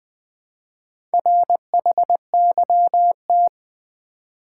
Morse code, heard as a keyed tone, about 20 wpm, 700 Hz. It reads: RHYT
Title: Build A Word - 2 Letters At A Time - 6-Letter Words - 20wpm